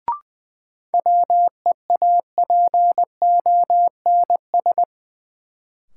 Morse code, heard as a keyed tone, about 20 wpm, 700 Hz. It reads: WEAPONS
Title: Sets of 5 Words Encore - New Word Drill - 20wpm